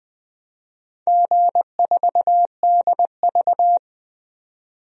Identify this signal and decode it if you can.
Morse code, heard as a keyed tone, about 20 wpm, 700 Hz. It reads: G4DV
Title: DXCC-100 Call Signs - 4 Characters - 20wpm